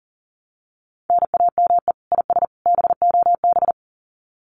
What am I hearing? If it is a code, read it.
Morse code, heard as a keyed tone, about 40 wpm, 700 Hz. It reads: NAME IS BOB
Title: Common QSO Phrases 40wpm